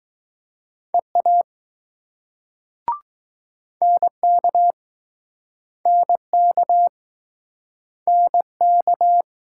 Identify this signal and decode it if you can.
Morse code, heard as a keyed tone, about 20 wpm, 700 Hz. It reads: EA NK NK NK